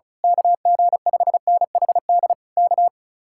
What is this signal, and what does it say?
Morse code, heard as a keyed tone, about 33 wpm, 700 Hz. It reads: KG5NHD K
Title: Speed-Racing Common QSO Phrases Encore - 35wpm